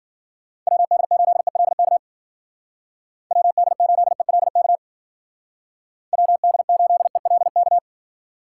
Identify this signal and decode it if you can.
Morse code, heard as a keyed tone, about 49 wpm, 700 Hz. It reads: WD8ELK WD8ELK WD8ELK
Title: Speed Racing - USA 2x3 Call Signs 40wpm